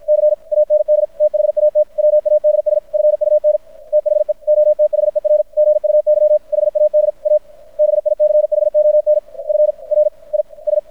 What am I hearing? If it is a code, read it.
Morse code, heard as a keyed tone, about 53 wpm, 600 Hz. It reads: C AND THAT WADS FUN THE OTHER GUY HAD A BICHON FRIS
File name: archived / CRSnet